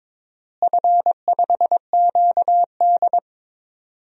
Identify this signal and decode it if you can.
Morse code, heard as a keyed tone, about 22 wpm, 700 Hz. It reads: F5QD
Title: DXCC-100 Call Signs - 4 Characters - Repeated 2x - 22wpm